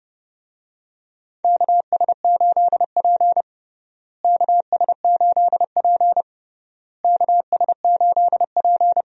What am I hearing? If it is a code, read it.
Morse code, heard as a keyed tone, about 30 wpm, 700 Hz. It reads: KS8P KS8P KS8P
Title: USA Call Signs - 4 Characters - Repeated 3x - 30wpm